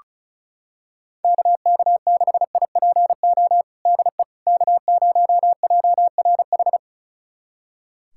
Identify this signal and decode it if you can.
Morse code, heard as a keyed tone, about 36 wpm, 700 Hz. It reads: KK6IPO DE K0JRH
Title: USA Callsign DE Callsign 35wpm